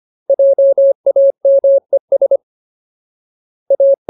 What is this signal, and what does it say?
Morse code, heard as a keyed tone, about 25 wpm, 550 Hz. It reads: JAMES A
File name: Ultimate Word List / 25WPM Ultimate